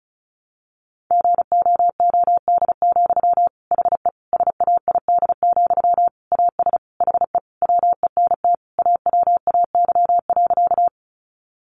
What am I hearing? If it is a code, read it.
Morse code, heard as a keyed tone, about 34 wpm, 700 Hz. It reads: GOOD, HE SAID, AS HE WENT AWAY.
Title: Sentences from Top 600 Words 35wpm